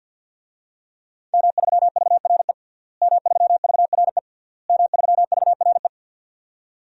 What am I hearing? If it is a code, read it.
Morse code, heard as a keyed tone, about 52 wpm, 700 Hz. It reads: M3VRE M3VRE M3VRE
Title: DXCC-100 Call Signs - 5 Characters - Repeated 3x - 50wpm